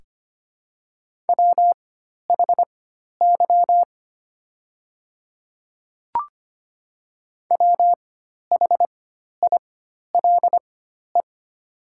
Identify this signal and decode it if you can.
Morse code, heard as a keyed tone, about 25 wpm, spaced as for 12 wpm, 690 Hz. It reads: WHY WHILE
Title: 4x Character Spacing - Top 300 Words - 25wpm